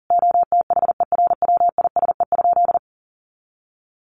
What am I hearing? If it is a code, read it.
Morse code, heard as a keyed tone, about 40 wpm, 700 Hz. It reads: OTHERWISE?
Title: Sentences from Top 1200 Words 40wpm